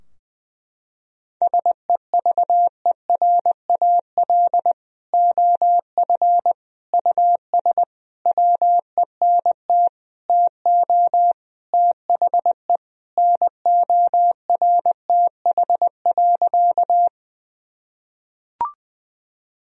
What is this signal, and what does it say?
Morse code, heard as a keyed tone, about 20 wpm, 700 Hz. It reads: SEVERAL OF US WENT TO THE NORTH.
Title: Sentences from Top 1000 Words - Review Part 2 - 20wpm